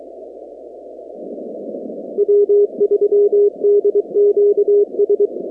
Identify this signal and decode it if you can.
Morse code, heard as a keyed tone, about 23 wpm, 410 Hz. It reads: W3DQS
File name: contests / 2019-cw-wpx-kt5j-k5pi- -2330 0-46